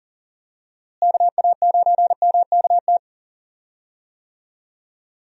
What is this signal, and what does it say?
Morse code, heard as a keyed tone, about 40 wpm, 700 Hz. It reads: KA9MKT
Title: USA Call Signs - 6 Characters - Repeated 2x - 40wpm